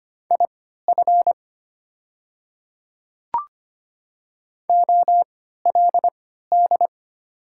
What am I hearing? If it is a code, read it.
Morse code, heard as a keyed tone, about 25 wpm, spaced as for 14 wpm, 700 Hz. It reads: IF OLD
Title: 3x Character Spacing - Top 200 Words - 25wpm